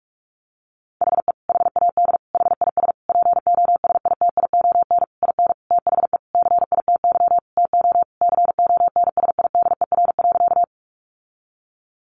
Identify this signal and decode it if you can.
Morse code, heard as a keyed tone, about 44 wpm, 700 Hz. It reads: HE HAD HIS POSITION IN THE CITY TO CONSIDER.
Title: Sentences from Top 900 Words 45wpm